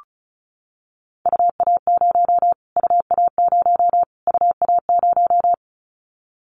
Morse code, heard as a keyed tone, about 34 wpm, 700 Hz. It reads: UA0 UA0 UA0
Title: Callsign Prefixes for Top 25 DXCC - Repeated 3x - 35wpm